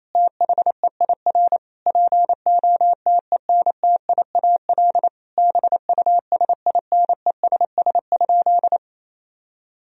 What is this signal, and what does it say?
Morse code, heard as a keyed tone, about 28 wpm, 700 Hz. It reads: THEIR POTENTIAL BUSINESS?
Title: Sentences from Top 1000 Words - Review Part 4 - 28wpm